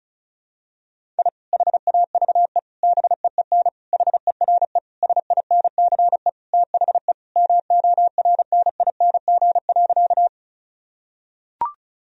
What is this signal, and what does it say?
Morse code, heard as a keyed tone, about 36 wpm, 700 Hz. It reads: I HAVE BEEN HERE SINCE THE MORNING.
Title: Sentences from Top 1000 Words - Review Part 1 - 35wpm